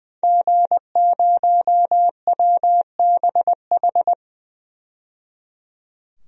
Morse code, heard as a keyed tone, about 20 wpm, 700 Hz. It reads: G0WBH
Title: DXCC-100 Call Signs - 5 Characters - 20wpm